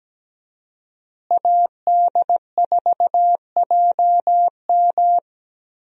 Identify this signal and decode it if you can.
Morse code, heard as a keyed tone, about 17 wpm, 700 Hz. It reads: AD4JM
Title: USA Call Signs - 5 Characters - Repeated 3x - 17wpm